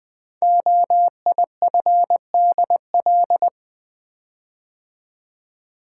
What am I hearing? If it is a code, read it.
Morse code, heard as a keyed tone, about 20 wpm, 700 Hz. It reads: OIFDL
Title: Code Groups - 5 Letters - 20wpm